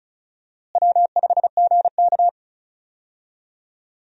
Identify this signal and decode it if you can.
Morse code, heard as a keyed tone, about 34 wpm, 700 Hz. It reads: W5GK
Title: USA Call Signs - 4 Characters - Repeated 3x - 35wpm